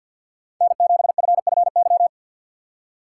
Morse code, heard as a keyed tone, about 46 wpm, 700 Hz. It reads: N7FFY
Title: USA Call Signs - 5 Characters - 4x Character Spacing - 50wpm